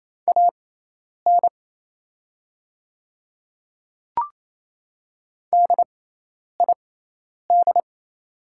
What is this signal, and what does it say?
Morse code, heard as a keyed tone, about 28 wpm, spaced as for 10 wpm, 700 Hz. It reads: AN DID
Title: 6x Character Spacing - Top 500 Words - 28wpm